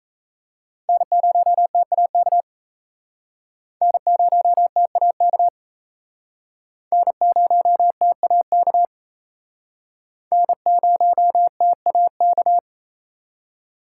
Morse code, heard as a keyed tone, about 37 wpm, 700 Hz. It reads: N0TAK N0TAK N0TAK N0TAK